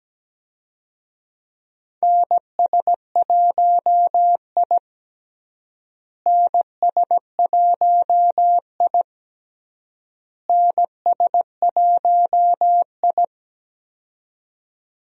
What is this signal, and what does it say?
Morse code, heard as a keyed tone, about 17 wpm, 700 Hz. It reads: NS1I NS1I NS1I
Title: USA Call Signs - 4 Characters - Repeated 3x - 17wpm